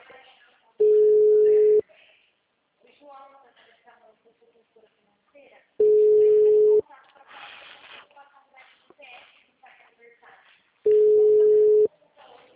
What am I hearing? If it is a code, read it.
Ringback tone (Europe, 425 Hz), with speech.
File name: out- -2017- - - .73430